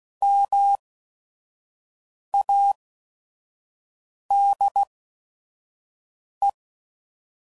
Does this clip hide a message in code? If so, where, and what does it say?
Yes: Morse code from 0.2 to 6.5 s, heard as a keyed tone, about 16 wpm, spaced as for 5 wpm, 800 Hz. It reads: MADE